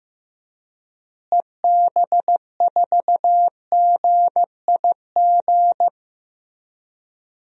Morse code, heard as a keyed tone, about 16 wpm, 700 Hz. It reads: EB4GIG
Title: DXCC-100 Call Signs - 6 Characters - Repeated 2x - 15wpm